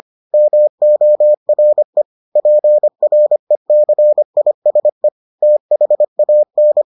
Morse code, heard as a keyed tone, about 25 wpm, 600 Hz. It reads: MORE PRECISE THAN